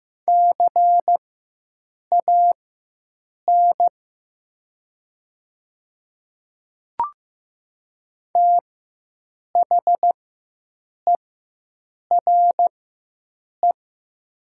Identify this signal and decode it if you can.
Morse code, heard as a keyed tone, about 15 wpm, spaced as for 7 wpm, 700 Hz. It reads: CAN THERE
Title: 4x Character Spacing - Top 100 Words - 15wpm